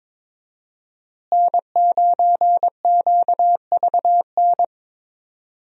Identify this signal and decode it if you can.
Morse code, heard as a keyed tone, about 22 wpm, 700 Hz. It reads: N9QVN
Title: USA Call Signs - 5 Characters - Repeated 2x - 22wpm